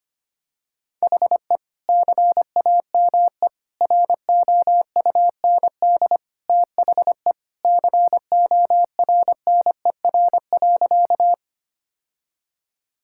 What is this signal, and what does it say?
Morse code, heard as a keyed tone, about 25 wpm, 700 Hz. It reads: HE CAME ROUND THE CORNER.
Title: Sentences from Top 1600 Words 25wpm